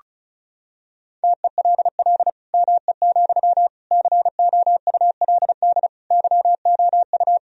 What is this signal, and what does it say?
Morse code, heard as a keyed tone, about 34 wpm, 700 Hz. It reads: TELL ME, COULD YOU
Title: Sentences from Top 1000 Words - Review Part 2 - 35wpm